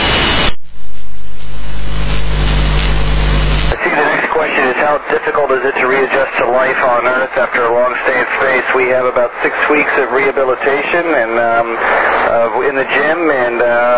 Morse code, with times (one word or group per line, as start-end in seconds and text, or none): none